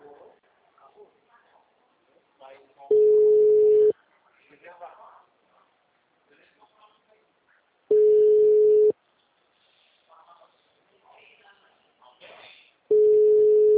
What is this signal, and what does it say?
Ringback tone (Europe, 425 Hz), with speech.
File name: out- -2016- - - .45127